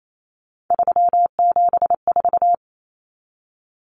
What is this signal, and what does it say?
Morse code, heard as a keyed tone, about 28 wpm, 700 Hz. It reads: 374
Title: Numbers - 3-Digit 28wpm